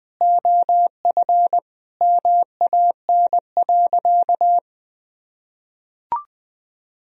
Morse code, heard as a keyed tone, about 20 wpm, 700 Hz. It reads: OF MAN.
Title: Sentences from Top 1000 Words - Review Part 2 - 20wpm